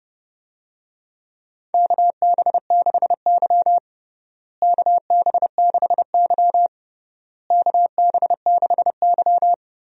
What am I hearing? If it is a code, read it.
Morse code, heard as a keyed tone, about 30 wpm, 700 Hz. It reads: KB6Y KB6Y KB6Y
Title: USA Call Signs - 4 Characters - Repeated 3x - 30wpm